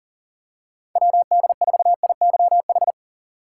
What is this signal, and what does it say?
Morse code, heard as a keyed tone, about 40 wpm, 700 Hz. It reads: WD4IYH
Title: USA Call Signs - 3x Character Spacing - 40wpm